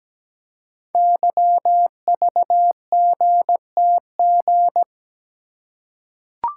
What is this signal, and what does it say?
Morse code, heard as a keyed tone, about 17 wpm, 700 Hz. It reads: YVGTG
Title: Code Groups - 5 Letters - 17wpm